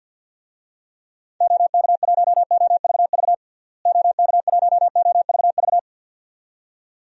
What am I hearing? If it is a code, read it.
Morse code, heard as a keyed tone, about 50 wpm, 700 Hz. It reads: OK1OVV OK1OVV
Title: DXCC-100 Call Signs - 6 Characters - Repeated 2x - 50wpm